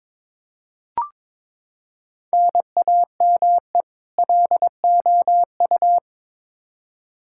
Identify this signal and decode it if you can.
Morse code, heard as a keyed tone, about 22 wpm, 700 Hz. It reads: NAME LOU
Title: Practice QSO - Exchange 1 - Beginning - 22wpm